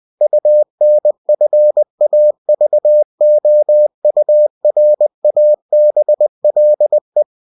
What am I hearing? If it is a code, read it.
Morse code, heard as a keyed tone, about 20 wpm, 600 Hz. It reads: UNFAVOURABLE
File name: cw-books / winnie-the-pooh / winnie-20wpm-page-0096